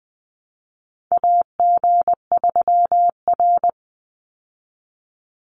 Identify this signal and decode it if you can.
Morse code, heard as a keyed tone, about 20 wpm, 700 Hz. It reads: AG3R
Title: Call Signs Repeated 3x - 20wpm